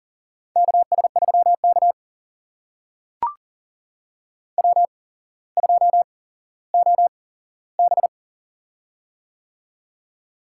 Morse code, heard as a keyed tone, about 40 wpm, spaced as for 17 wpm, 700 Hz. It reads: KS3K W2OB
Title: USA Call Signs - 4 Characters - 8x Character Spacing - 40wpm